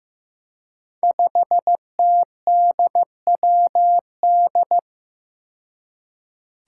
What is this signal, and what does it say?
Morse code, heard as a keyed tone, about 15 wpm, 700 Hz. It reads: 5TDWD
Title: Five Challenging Letter-Number Combinations 15wpm